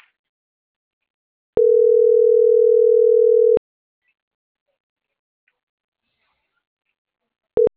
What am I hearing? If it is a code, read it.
Ringback tone (North America), 440 + 480 Hz.